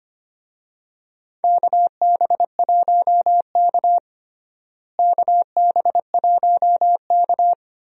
Morse code, heard as a keyed tone, about 25 wpm, 700 Hz. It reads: KB1K KB1K
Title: USA Call Signs - 4 Characters - Repeated 2x - 25wpm